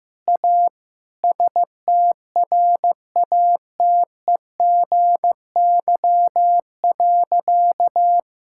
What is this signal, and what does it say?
Morse code, heard as a keyed tone, about 15 wpm, 700 Hz. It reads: A STRATEGY.
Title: Sentences from Top 1000 Words - Review Part 5 - 15wpm